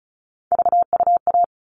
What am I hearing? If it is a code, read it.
Morse code, heard as a keyed tone, about 34 wpm, 700 Hz. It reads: VUA
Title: All Cut Numbers - 3-Digit 35wpm